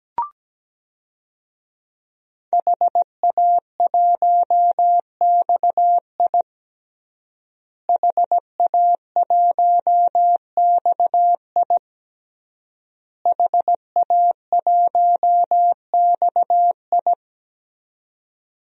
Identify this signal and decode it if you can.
Morse code, heard as a keyed tone, about 17 wpm, 700 Hz. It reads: HA1XI HA1XI HA1XI
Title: DXCC-100 Call Signs - 5 Characters - Repeated 3x - 17wpm